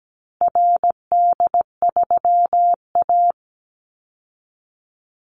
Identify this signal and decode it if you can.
Morse code, heard as a keyed tone, about 17 wpm, 700 Hz. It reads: RD3A
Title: Call Signs Repeated 2x - 17wpm